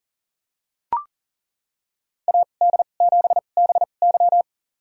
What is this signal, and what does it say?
Morse code, heard as a keyed tone, about 37 wpm, spaced as for 28 wpm, 700 Hz. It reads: AD7BY
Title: USA Call Signs - 2x Character Spacing - 40wpm